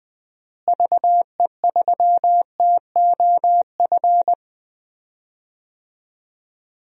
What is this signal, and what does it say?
Morse code, heard as a keyed tone, about 20 wpm, 700 Hz. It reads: VE3TOF
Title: DXCC-100 Call Signs - 6 Characters - Repeated 3x - 20wpm